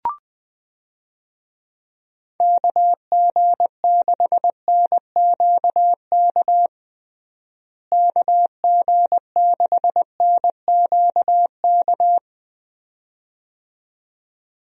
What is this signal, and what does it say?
Morse code, heard as a keyed tone, about 20 wpm, 700 Hz. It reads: KG6NQK KG6NQK
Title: USA Call Signs - 6 Characters - Repeated 2x - 20wpm